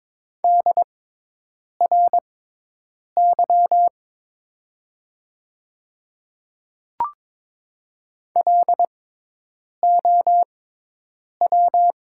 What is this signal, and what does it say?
Morse code, heard as a keyed tone, about 22 wpm, spaced as for 8 wpm, 700 Hz. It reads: DRY LOW
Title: 6x Character Spacing - Top 500 Words - 22wpm